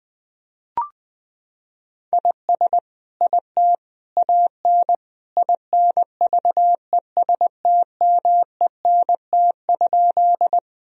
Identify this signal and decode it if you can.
Morse code, heard as a keyed tone, about 20 wpm, 700 Hz. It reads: IS IT AN INVESTMENT?
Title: Sentences from Top 1000 Words - Review Part 4 - 20wpm